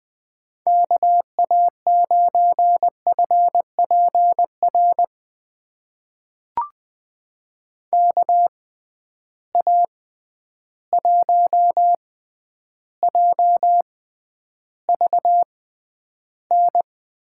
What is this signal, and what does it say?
Morse code, heard as a keyed tone, about 21 wpm, spaced as for 10 wpm, 700 Hz. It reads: KA9FPR KA1JVN
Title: USA Call Signs - 6 Characters - 6x Character Spacing - 20wpm